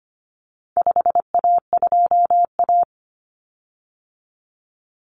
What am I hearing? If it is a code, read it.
Morse code, heard as a keyed tone, about 25 wpm, 700 Hz. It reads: 5A2A
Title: Common Cut Numbers - 4-Digit 25wpm